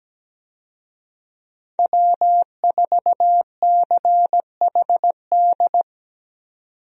Morse code, heard as a keyed tone, about 17 wpm, 700 Hz. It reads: W4CHD
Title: USA Call Signs - 5 Characters - 17wpm